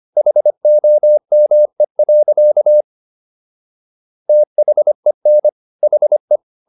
Morse code, heard as a keyed tone, about 25 wpm, 600 Hz. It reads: HOME. THEN HE